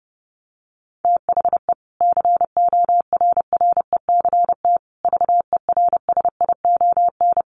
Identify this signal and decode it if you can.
Morse code, heard as a keyed tone, about 30 wpm, 700 Hz. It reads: THE CORRECT VERSION
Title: Sentences from Top 1400 Words 30wpm